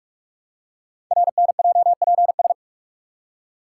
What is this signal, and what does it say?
Morse code, heard as a keyed tone, about 44 wpm, 700 Hz. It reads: RNJPS
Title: Code Groups - 5 Letters - 45wpm